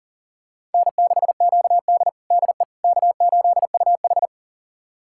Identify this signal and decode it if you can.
Morse code, heard as a keyed tone, about 40 wpm, 700 Hz. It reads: N6QD DE K8UH
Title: USA Callsign DE Callsign - 4 Characters - 40wpm